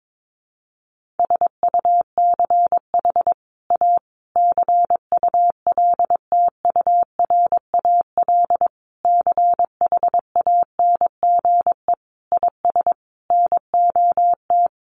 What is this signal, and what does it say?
Morse code, heard as a keyed tone, about 22 wpm, 700 Hz. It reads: SUCH A CULTURAL CHANGE IS NOT
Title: Sentences from Top 1500 Words 22wpm